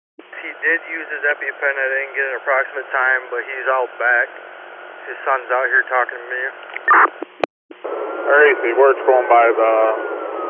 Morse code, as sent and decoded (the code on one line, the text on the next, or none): none